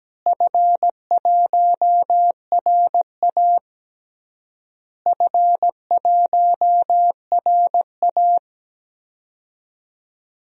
Morse code, heard as a keyed tone, about 17 wpm, 700 Hz. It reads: F1RA F1RA
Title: DXCC-100 Call Signs - 4 Characters - Repeated 3x - 17wpm